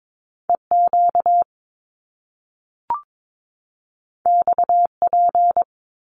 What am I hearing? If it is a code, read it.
Morse code, heard as a keyed tone, about 22 wpm, 700 Hz. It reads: EQ XP
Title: Top Two Letter Combinations in Words 22wpm